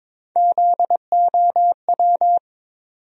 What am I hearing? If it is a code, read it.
Morse code, heard as a keyed tone, about 22 wpm, 700 Hz. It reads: ZOW
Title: USA Call Sign Postfixes - 3 Letters 22wpm